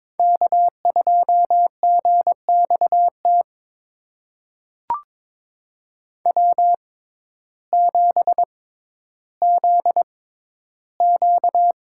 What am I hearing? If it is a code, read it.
Morse code, heard as a keyed tone, about 22 wpm, spaced as for 12 wpm, 700 Hz. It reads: K2GXT W7ZQ
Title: USA Call Signs - 6x Character Spacing - 22wpm